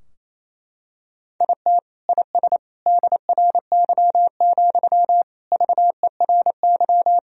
Morse code, heard as a keyed tone, about 28 wpm, 700 Hz. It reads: IT IS DRY, VERY
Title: Sentences from Top 1000 Words - Review Part 3 - 28wpm